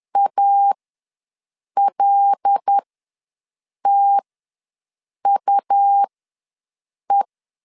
Morse code, heard as a keyed tone, about 10 wpm, spaced as for 6 wpm, 800 Hz. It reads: ALTUE
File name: LEZIONI / 7 GF9Z / 7 G ws 10wpm 3843